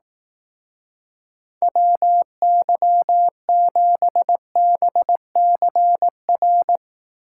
Morse code, heard as a keyed tone, about 18 wpm, 700 Hz. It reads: WY7BCR
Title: Speed Racing - USA 2x3 Call Signs 15wpm